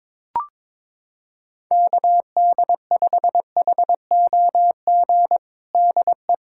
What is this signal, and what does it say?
Morse code, heard as a keyed tone, about 23 wpm, 700 Hz. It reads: KD5HOG DE
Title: USA Callsign DE Callsign - 6 Characters - 22wpm